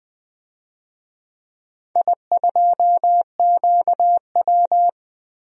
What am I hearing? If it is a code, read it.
Morse code, heard as a keyed tone, about 20 wpm, 700 Hz. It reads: I2QW
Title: DXCC-100 Call Signs - 4 Characters - Repeated 3x - 20wpm